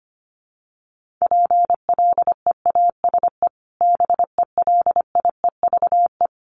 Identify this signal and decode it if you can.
Morse code, heard as a keyed tone, about 25 wpm, 700 Hz. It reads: PLEASE BELIEVE
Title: Sentences from Top 1500 Words 25wpm